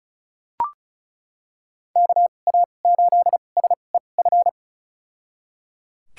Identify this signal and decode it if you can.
Morse code, heard as a keyed tone, about 36 wpm, spaced as for 25 wpm, 700 Hz. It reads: KA8SEF
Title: USA Call Signs - 6 Characters - 2x Character Spacing - 35wpm